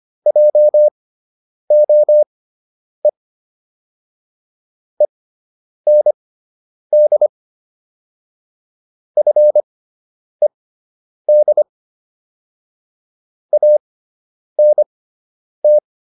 Morse code, heard as a keyed tone, about 25 wpm, spaced as for 9 wpm, 600 Hz. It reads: JOE END FED ANT